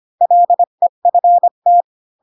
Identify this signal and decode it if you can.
Morse code, heard as a keyed tone, about 25 wpm, 700 Hz. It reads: LEFT